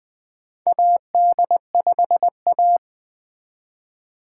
Morse code, heard as a keyed tone, about 20 wpm, 700 Hz. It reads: AD5A
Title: 2023 SOTA Chaser Call Signs - 20wpm